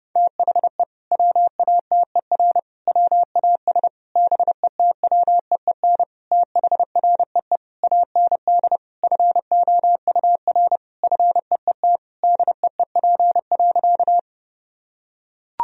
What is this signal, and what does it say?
Morse code, heard as a keyed tone, about 30 wpm, 700 Hz. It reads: THE WATER WAS BETWEEN THREE AND FOUR FEET DEEP.
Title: Sentences from Top 1000 Words - Review Part 3 - 30wpm